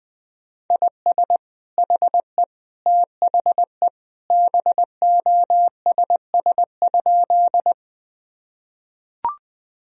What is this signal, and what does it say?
Morse code, heard as a keyed tone, about 20 wpm, 700 Hz. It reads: IS HE THE BOSS?
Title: Sentences from Top 1000 Words - Review Part 4 - 20wpm